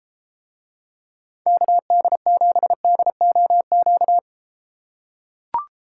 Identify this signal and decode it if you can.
Morse code, heard as a keyed tone, about 32 wpm, 700 Hz. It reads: KD7DOQ